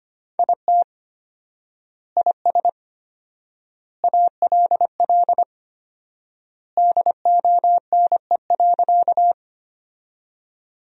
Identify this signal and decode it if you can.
Morse code, heard as a keyed tone, about 25 wpm, 700 Hz. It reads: IT IS ALL DONE.
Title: 4x Word-Spacing - Sentences from Top 500 Words - 25wpm